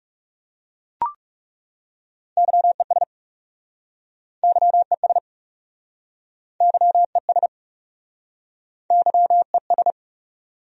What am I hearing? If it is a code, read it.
Morse code, heard as a keyed tone, about 40 wpm, 700 Hz. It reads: YES YES YES YES